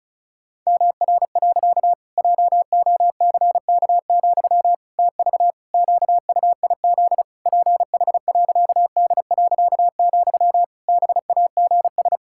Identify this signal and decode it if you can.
Morse code, heard as a keyed tone, about 36 wpm, 700 Hz. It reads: MR. JOCK, TV QUIZ PH.D., BAGS
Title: Pangram Challenge v2 - 35wpm